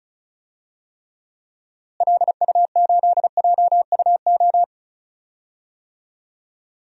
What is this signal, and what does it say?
Morse code, heard as a keyed tone, about 36 wpm, 700 Hz. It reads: LU8JUO
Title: DXCC-100 Call Signs - 6 Characters - Repeated 3x - 35wpm